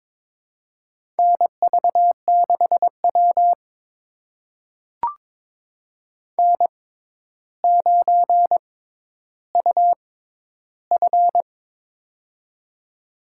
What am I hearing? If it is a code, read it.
Morse code, heard as a keyed tone, about 22 wpm, spaced as for 11 wpm, 700 Hz. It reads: NV6W N9UF
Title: USA Call Signs - 4 Characters - 6x Character Spacing - 22wpm